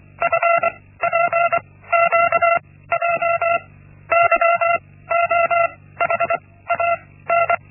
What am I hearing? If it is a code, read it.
Morse code, heard as a keyed tone, about 24 wpm, spaced as for 16 wpm, 680 Hz. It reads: FPQJ YOHAN
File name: Linksmakalnis-1